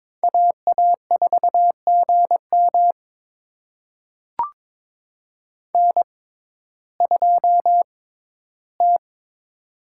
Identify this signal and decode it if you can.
Morse code, heard as a keyed tone, about 22 wpm, spaced as for 13 wpm, 700 Hz. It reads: AA4GM N2T
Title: USA Call Signs - 5 Characters - 6x Character Spacing - 22wpm